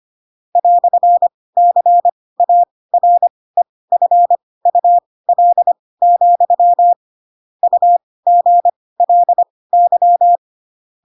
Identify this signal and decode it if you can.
Morse code, heard as a keyed tone, about 25 wpm, spaced as for 18 wpm, 700 Hz. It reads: "CAREFUL, UGLY
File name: Pinocchio / Pinocchio ch4 12 25 ik1hge